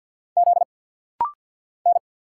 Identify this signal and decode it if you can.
Morse code, heard as a keyed tone, about 50 wpm, 700 Hz. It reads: Z N
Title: Single Letters - Rapid-Fire 50wpm